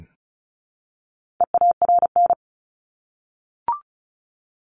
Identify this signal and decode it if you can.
Morse code, heard as a keyed tone, about 34 wpm, 700 Hz. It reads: EARN
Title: Top 1251-1500 Words 35wpm